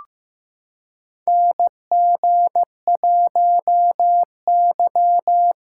Morse code, heard as a keyed tone, about 15 wpm, 700 Hz. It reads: NG1Y
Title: USA Callsign DE Callsign - 4 Characters - 15wpm